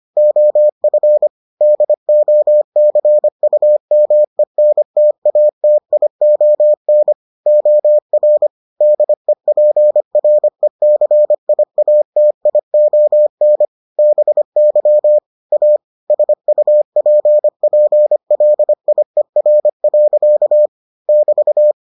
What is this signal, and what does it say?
Morse code, heard as a keyed tone, about 25 wpm, 600 Hz. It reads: OF DOCUMENTATION OR DEPRECIATION BY A SUPPLIER. <BT>